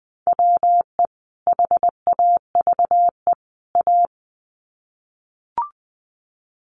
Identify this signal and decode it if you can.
Morse code, heard as a keyed tone, about 19 wpm, 700 Hz. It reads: WE HAVE A
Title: Sets of 3 Words 20wpm